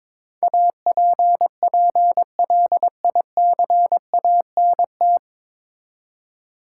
Morse code, heard as a keyed tone, about 22 wpm, 700 Hz. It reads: APPLICANT
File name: 9 Letter Words - First 3 Letters Given - 22wpm